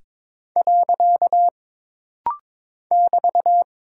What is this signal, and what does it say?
Morse code, heard as a keyed tone, about 22 wpm, 700 Hz. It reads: . <BT>
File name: practice / 22wpm / Single Punctuation-Prosign - Rapid-Fire - 22wpm